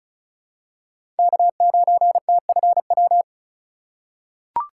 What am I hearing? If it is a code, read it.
Morse code, heard as a keyed tone, about 34 wpm, 700 Hz. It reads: K9TFW
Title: USA Call Signs - 5 Characters - 2x Character Spacing - 35wpm